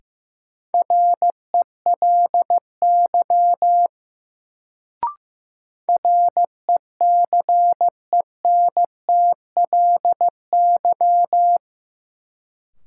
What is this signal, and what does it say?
Morse code, heard as a keyed tone, about 15 wpm, 700 Hz. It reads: RELY RECENTLY
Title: Sentences from Top 1200 Words - New Word Drill - 15wpm